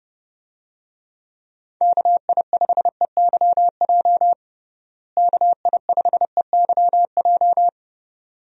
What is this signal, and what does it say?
Morse code, heard as a keyed tone, about 30 wpm, 700 Hz. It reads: KI5EYJ KI5EYJ
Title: USA Call Signs - 6 Characters - Repeated 2x - 30wpm